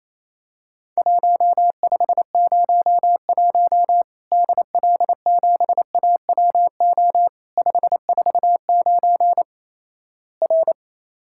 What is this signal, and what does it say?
Morse code, heard as a keyed tone, about 28 wpm, 700 Hz. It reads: 1501 DL7AWO 549 R
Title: Worked All Europe DX Contest - QTC Exchange - 28wpm